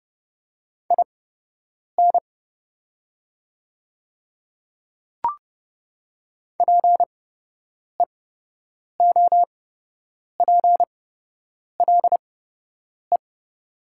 Morse code, heard as a keyed tone, about 30 wpm, spaced as for 8 wpm, 700 Hz. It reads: IN PEOPLE
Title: 8x Character Spacing - Top 100 Words - 30wpm